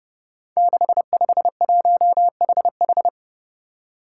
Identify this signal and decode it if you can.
Morse code, heard as a keyed tone, about 28 wpm, 700 Hz. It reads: 651HH
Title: Five Challenging Letter-Number Combinations 30wpm